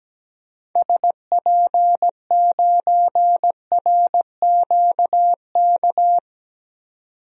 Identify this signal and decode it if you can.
Morse code, heard as a keyed tone, about 17 wpm, 700 Hz. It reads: SP9RQK
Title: DXCC-100 Call Signs - 6 Characters - 17wpm